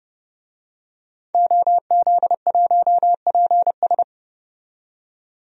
Morse code, heard as a keyed tone, about 32 wpm, 700 Hz. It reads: OZ1PS
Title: DXCC-100 Call Signs - 5 Characters - 30wpm